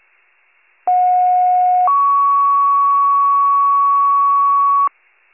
Two-tone page, heard as a one-second tone, then a three-second tone, about 730 then 1120 Hz. It is heard from 0.9 to 4.9 s.
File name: pager / 94 RESCUE OFC / wcalert